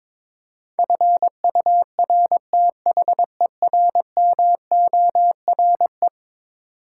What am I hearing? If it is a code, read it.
Morse code, heard as a keyed tone, about 21 wpm, 700 Hz. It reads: FURTHERMORE
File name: practice / 22wpm / Eleven Letter Words - 22wpm